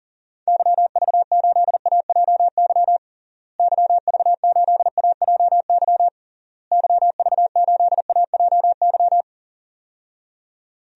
Morse code, heard as a keyed tone, about 37 wpm, 700 Hz. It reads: YV8AJY YV8AJY YV8AJY
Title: DXCC-100 Call Signs - 6 Characters - Repeated 3x - 40wpm